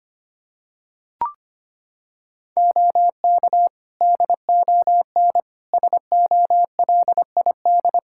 Morse code, heard as a keyed tone, about 25 wpm, 700 Hz. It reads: OK DON SOLID
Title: Practice QSO - Exchange 3 - More Info - 25wpm